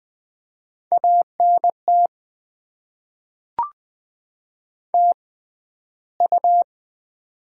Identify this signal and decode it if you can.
Morse code, heard as a keyed tone, about 20 wpm, spaced as for 12 wpm, 700 Hz. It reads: ANT TU
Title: Top 20 QSO Elements - 6x Character Spacing - 20wpm